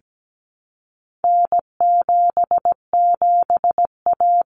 Morse code, heard as a keyed tone, about 17 wpm, 700 Hz. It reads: N77A
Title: Common Cut Numbers - 4-Digit 17wpm